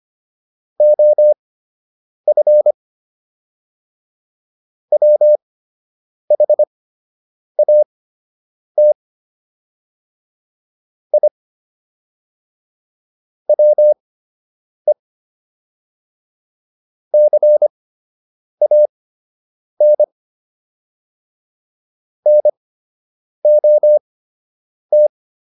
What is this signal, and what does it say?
Morse code, heard as a keyed tone, about 25 wpm, spaced as for 8 wpm, 600 Hz. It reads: OF WHAT I WE CAN NOT